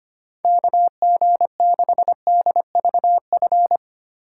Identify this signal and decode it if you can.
Morse code, heard as a keyed tone, about 25 wpm, 700 Hz. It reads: KG6DVF
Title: USA Call Signs - 6 Characters - 3x Character Spacing - 25wpm